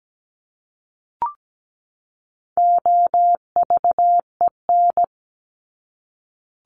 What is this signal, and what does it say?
Morse code, heard as a keyed tone, about 17 wpm, 700 Hz. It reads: OVEN
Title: Top 501-750 Words 17wpm